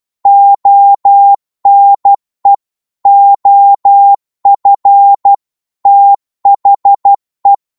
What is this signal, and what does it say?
Morse code, heard as a keyed tone, about 12 wpm, 800 Hz. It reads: ONE OF THE
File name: cwmp3 / psg-1-12wpm